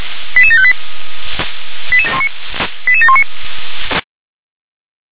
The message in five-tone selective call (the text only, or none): 88758 88719